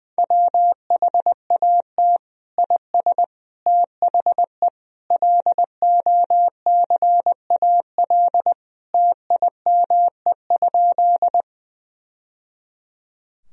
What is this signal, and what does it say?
Morse code, heard as a keyed tone, about 20 wpm, 700 Hz. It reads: WHAT IS THE LOCAL TIME?
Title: Sentences from Top 1000 Words - Review Part 3 - 20wpm